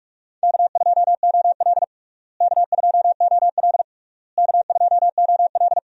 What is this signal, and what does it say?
Morse code, heard as a keyed tone, about 46 wpm, 700 Hz. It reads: K2OL K2OL K2OL
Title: USA Call Signs - 4 Characters - Repeated 3x - 45wpm